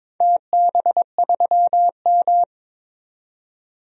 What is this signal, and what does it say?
Morse code, heard as a keyed tone, about 22 wpm, 700 Hz. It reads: TB3M
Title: DXCC-100 Call Signs - 4 Characters - 22wpm